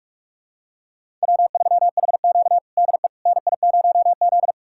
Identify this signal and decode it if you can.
Morse code, heard as a keyed tone, about 46 wpm, 700 Hz. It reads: W3HQ DE NI0Z
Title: USA Callsign DE Callsign - 4 Characters - 45wpm